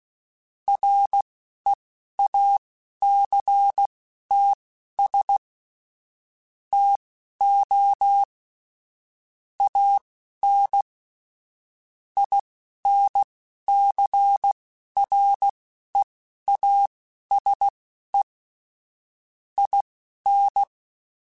Morse code, heard as a keyed tone, about 16 wpm, spaced as for 12 wpm, 800 Hz. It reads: REACTS TO AN INCREASE IN